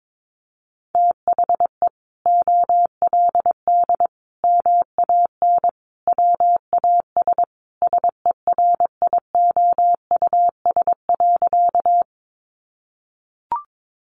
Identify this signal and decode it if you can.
Morse code, heard as a keyed tone, about 22 wpm, 700 Hz. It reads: THE OLD MAN WAS SERIOUS.
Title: Sentences from Top 1100 Words 22wpm